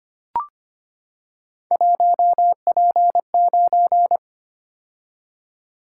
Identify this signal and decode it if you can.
Morse code, heard as a keyed tone, about 25 wpm, 700 Hz. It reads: 1P9
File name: Three Letter-Number Combinations - 25wpm